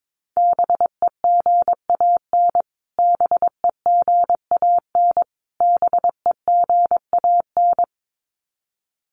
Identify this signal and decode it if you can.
Morse code, heard as a keyed tone, about 22 wpm, 700 Hz. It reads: BEGAN BEGAN BEGAN
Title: Top 300 Words Repeated 22wpm